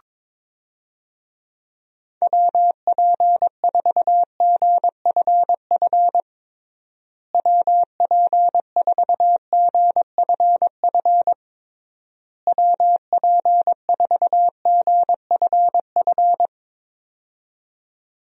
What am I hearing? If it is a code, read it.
Morse code, heard as a keyed tone, about 22 wpm, 700 Hz. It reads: WP4GFF WP4GFF WP4GFF
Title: USA Call Signs - 6 Characters - Repeated 3x - 22wpm